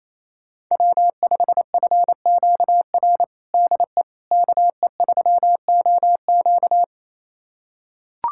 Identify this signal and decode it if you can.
Morse code, heard as a keyed tone, about 28 wpm, 700 Hz. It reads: W5FQR DE KE3OQ
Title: USA Callsign DE Callsign - 5 Characters - 28wpm